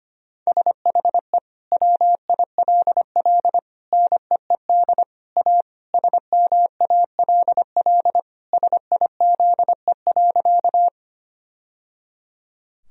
Morse code, heard as a keyed tone, about 25 wpm, 700 Hz. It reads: SHE WILL NEED A SMALL SIZE.
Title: Sentences from Top 1000 Words - Review Part 2 - 25wpm